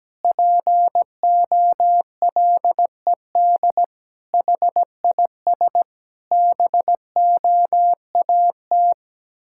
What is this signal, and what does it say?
Morse code, heard as a keyed tone, about 17 wpm, 700 Hz. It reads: POLED HIS BOAT
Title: Pangram Challenge v2 - 17wpm